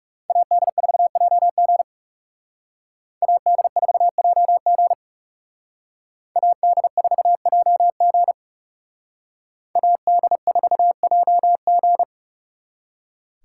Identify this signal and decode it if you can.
Morse code, heard as a keyed tone, about 36 wpm, 700 Hz. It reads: AD4JG AD4JG AD4JG AD4JG
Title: Speed Racing - USA 2x2 Call Signs 30wpm